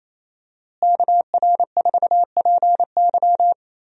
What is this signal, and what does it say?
Morse code, heard as a keyed tone, about 28 wpm, 700 Hz. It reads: KR4PY
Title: USA Call Signs - 5 Characters - Repeated 2x - 28wpm